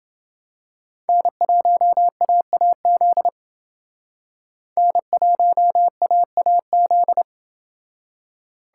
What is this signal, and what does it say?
Morse code, heard as a keyed tone, about 28 wpm, 700 Hz. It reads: N1AAZ N1AAZ